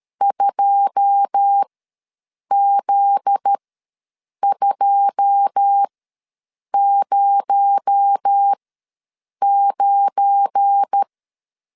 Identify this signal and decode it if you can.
Morse code, heard as a keyed tone, about 13 wpm, spaced as for 7 wpm, 800 Hz. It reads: 2Z209
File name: LEZIONI / 9 2 ws 12wpm 3406